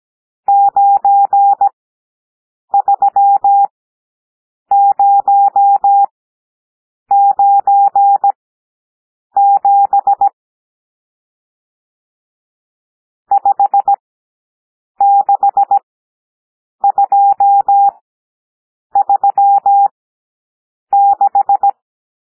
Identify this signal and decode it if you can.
Morse code, heard as a keyed tone, about 17 wpm, spaced as for 7 wpm, 800 Hz. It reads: 93097 56236
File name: SOUND NUMBERS 17 wpm